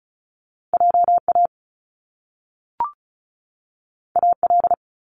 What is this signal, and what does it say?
Morse code, heard as a keyed tone, about 36 wpm, 700 Hz. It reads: JA AL
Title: Top Two Letter Combinations in Words 35wpm